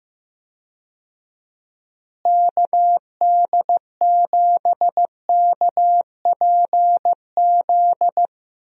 Morse code, heard as a keyed tone, about 15 wpm, 700 Hz. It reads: KD7KPZ
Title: USA Call Signs - 6 Characters - Repeated 2x - 15wpm